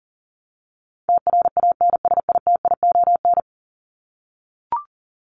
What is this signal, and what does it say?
Morse code, heard as a keyed tone, about 40 wpm, 700 Hz. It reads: TRANSITION
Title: Top 1751-2000 Words 40wpm